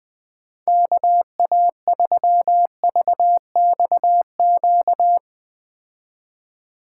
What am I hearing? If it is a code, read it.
Morse code, heard as a keyed tone, about 20 wpm, 700 Hz. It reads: KA3VXQ
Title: Speed Racing - USA Call Signs 17wpm